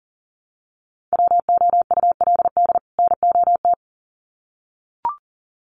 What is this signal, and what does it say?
Morse code, heard as a keyed tone, about 40 wpm, 700 Hz. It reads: WOULD NOT
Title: Sets of 2 Words 40wpm